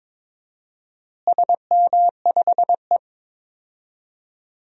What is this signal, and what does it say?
Morse code, heard as a keyed tone, about 22 wpm, 700 Hz. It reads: SM5E
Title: DXCC-100 Call Signs - 4 Characters - Repeated 2x - 22wpm